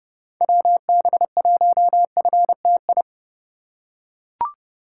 Morse code, heard as a keyed tone, about 30 wpm, 700 Hz. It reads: WB1FTI
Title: USA Call Signs - 6 Characters - 3x Character Spacing - 30wpm